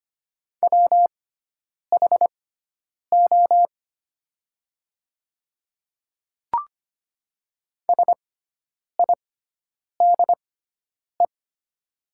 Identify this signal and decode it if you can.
Morse code, heard as a keyed tone, about 25 wpm, spaced as for 9 wpm, 690 Hz. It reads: WHO SIDE
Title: 6x Character Spacing - Top 500 Words - 25wpm